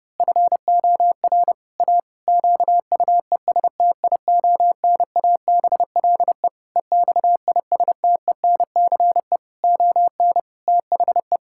Morse code, heard as a keyed tone, about 30 wpm, 700 Hz. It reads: FOR A QUESTIONABLE EXISTENCE ON THE
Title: Pangram Challenge v2 - 30wpm